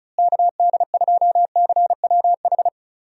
Morse code, heard as a keyed tone, about 34 wpm, 700 Hz. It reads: KD2CWH
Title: USA Call Signs - 6 Characters - Repeated 3x - 35wpm